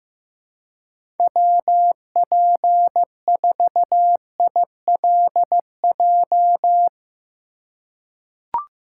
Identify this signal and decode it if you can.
Morse code, heard as a keyed tone, about 15 wpm, 700 Hz. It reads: WP4ILJ
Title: USA Call Signs - 6 Characters - 3x Character Spacing - 15wpm